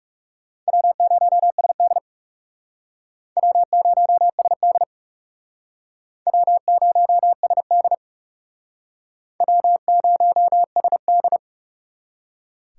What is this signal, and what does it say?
Morse code, heard as a keyed tone, about 36 wpm, 700 Hz. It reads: W0SD W0SD W0SD W0SD
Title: Speed Racing - USA 1x2 Call Signs 30wpm